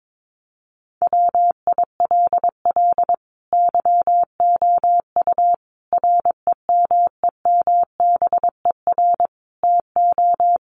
Morse code, heard as a keyed tone, about 22 wpm, 700 Hz. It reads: WILL YOU REMEMBER TO
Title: Sentences from Top 1100 Words 22wpm